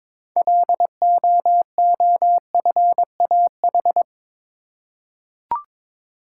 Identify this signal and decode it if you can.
Morse code, heard as a keyed tone, about 22 wpm, 700 Hz. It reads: LOOFAH
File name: Difficult Words with Hints - 22wpm